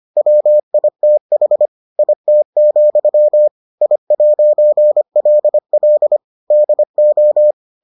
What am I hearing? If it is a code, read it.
Morse code, heard as a keyed tone, about 25 wpm, 600 Hz. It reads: WITH IT, I'LL DO